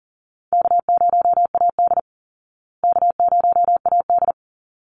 Morse code, heard as a keyed tone, about 40 wpm, 700 Hz. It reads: K0AD K0AD
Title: Easy Call Signs Repeated 2x - 40wpm